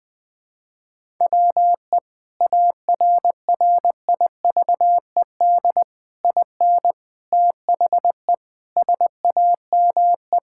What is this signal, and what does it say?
Morse code, heard as a keyed tone, about 20 wpm, 700 Hz. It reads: WE ARRIVED IN THE SAME
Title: Sentences with 8-Word Phrases - 20wpm